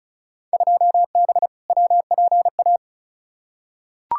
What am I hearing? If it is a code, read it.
Morse code, heard as a keyed tone, about 34 wpm, 700 Hz. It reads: 2B WPA
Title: Field Day Exchanges Encore - 35wpm